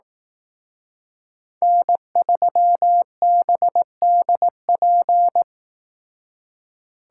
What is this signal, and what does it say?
Morse code, heard as a keyed tone, about 18 wpm, 700 Hz. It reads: N3BDP
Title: Speed Racing - USA 1x3 Call Signs 15wpm